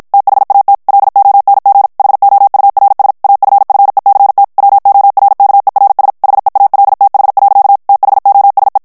Morse code, heard as a keyed tone, about 51 wpm, 800 Hz. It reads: THAT LONG HOURS AFFECT WORKERS HEALTH. THOSE
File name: morse mp3 / ANGLAIS texte clair 1 vitesse 50 mots minute